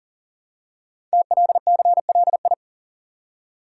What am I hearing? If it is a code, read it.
Morse code, heard as a keyed tone, about 39 wpm, 700 Hz. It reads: TLCLI
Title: Code Groups - 5 Letter:Numbers - 40wpm